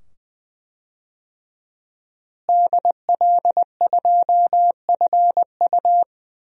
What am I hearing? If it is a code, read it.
Morse code, heard as a keyed tone, about 20 wpm, 700 Hz. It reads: DL2FU
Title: DXCC-100 Call Signs - 5 Characters - Repeated 2x - 20wpm